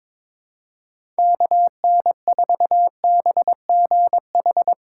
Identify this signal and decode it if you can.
Morse code, heard as a keyed tone, about 22 wpm, 700 Hz. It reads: KN4BGH
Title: USA Call Signs - 6 Characters - 3x Character Spacing - 22wpm